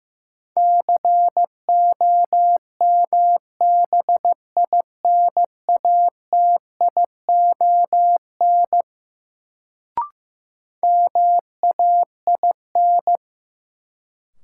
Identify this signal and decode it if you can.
Morse code, heard as a keyed tone, about 15 wpm, 700 Hz. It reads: COMBINATION MAIN
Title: Sentences from Top 1300 Words - New Word Drill Alt - 15wpm